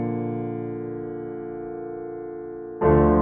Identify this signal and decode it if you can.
Dial tone (North America), 350 + 440 Hz.